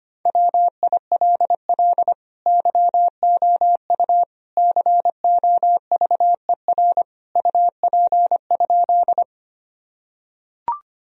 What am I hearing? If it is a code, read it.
Morse code, heard as a keyed tone, about 25 wpm, 700 Hz. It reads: WILL YOU COVER UP?
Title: Sentences from Top 1000 Words - Review Part 1 - 25wpm